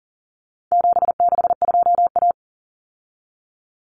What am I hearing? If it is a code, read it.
Morse code, heard as a keyed tone, about 40 wpm, 700 Hz. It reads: 762A
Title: Common Cut Numbers - 4-Digit 40wpm